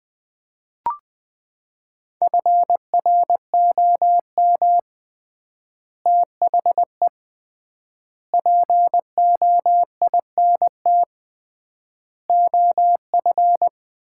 Morse code, heard as a keyed tone, about 20 wpm, 700 Hz. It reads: FROM THE POINT OF
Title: 3x Word-Spacing - Sets of 5 Words - 20wpm